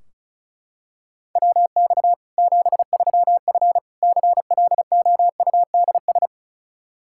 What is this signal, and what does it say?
Morse code, heard as a keyed tone, about 34 wpm, 700 Hz. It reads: WX 73F CLOUDS
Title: Practice QSO - Exchange 2 - Info - 35wpm